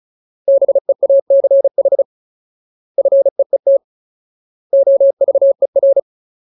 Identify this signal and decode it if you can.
Morse code, heard as a keyed tone, about 34 wpm, 550 Hz. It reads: BEACH FEET OVER
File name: Ultimate Word List / 35WPM Ultimate